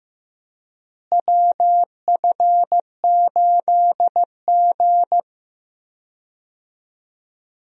Morse code, heard as a keyed tone, about 15 wpm, 700 Hz. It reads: WF8G
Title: USA Call Signs - 4 Characters - Repeated 3x - 15wpm